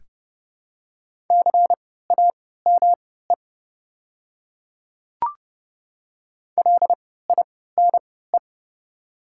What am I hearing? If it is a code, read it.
Morse code, heard as a keyed tone, about 30 wpm, spaced as for 17 wpm, 690 Hz. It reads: CAME LINE
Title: 3x Character Spacing - Top 300 Words - 30wpm